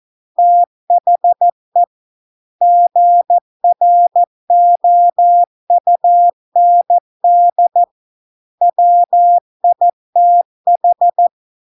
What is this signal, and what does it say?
Morse code, heard as a keyed tone, about 14 wpm, 700 Hz. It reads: THE GROUND WITH